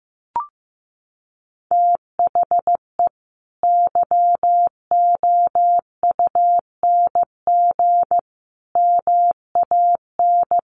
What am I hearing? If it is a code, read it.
Morse code, heard as a keyed tone, about 15 wpm, 700 Hz. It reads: THE YOUNG MAN
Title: Sentences from Top 1400 Words 15wpm